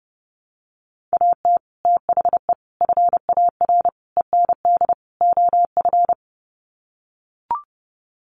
Morse code, heard as a keyed tone, about 30 wpm, 700 Hz. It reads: AT THE FAR END OF
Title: Sets of 5 Words 30wpm